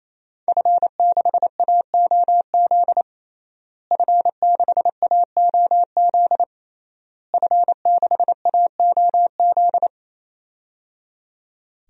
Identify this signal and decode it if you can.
Morse code, heard as a keyed tone, about 28 wpm, 700 Hz. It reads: F6AOZ F6AOZ F6AOZ
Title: DXCC-100 Call Signs - 5 Characters - Repeated 3x - 28wpm